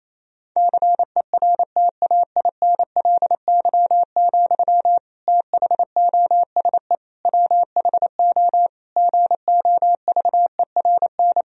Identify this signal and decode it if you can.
Morse code, heard as a keyed tone, about 28 wpm, 700 Hz. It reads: CERTAINLY, THOSE WHO GOVERN
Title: Sentences from Top 1000 Words - Review Part 2 - 28wpm